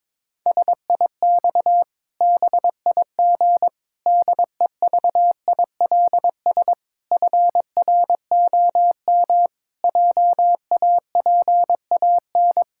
Morse code, heard as a keyed tone, about 22 wpm, 700 Hz. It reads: SIX BIG DEVILS FROM JAPAN
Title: Pangram Challenge v2 - 22wpm